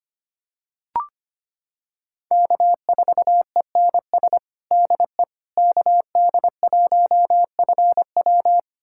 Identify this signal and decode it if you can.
Morse code, heard as a keyed tone, about 25 wpm, 700 Hz. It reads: K4ENS DE KD1FW
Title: USA Callsign DE Callsign 25wpm